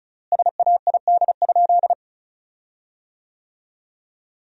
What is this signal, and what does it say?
Morse code, heard as a keyed tone, about 36 wpm, 700 Hz. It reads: SAID?
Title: 4x Word-Spacing - Sentences from Top 100 Words - 35wpm